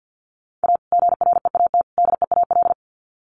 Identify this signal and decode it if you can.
Morse code, heard as a keyed tone, about 50 wpm, 700 Hz. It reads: A GREAT DEAL
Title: Sets of 3 Words 50wpm